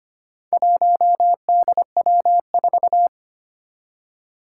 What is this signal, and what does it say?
Morse code, heard as a keyed tone, about 25 wpm, 700 Hz. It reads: 1DW4
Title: Four Challenging Letter-Number Combinations 25wpm